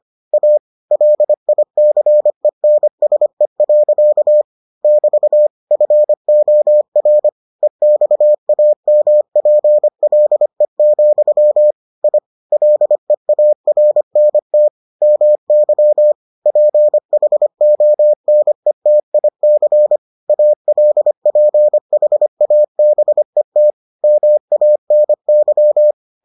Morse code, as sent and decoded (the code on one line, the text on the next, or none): .- / .-.. .. -.-. -. ... .-.-.- / -...- / ..-. --- .-. / . -..- .- -- .--. .-.. --..-- / .. / .-.. .- .-. -. - / -- -.-- / .--. .... --- -. - .. -.-. / .- .-.. .--. .... .- -... - / -- .- -. -.--
A LICENSE. <BT> FOR EXAMPLE, I LEARNT MY PHONETIC ALPHABET MANY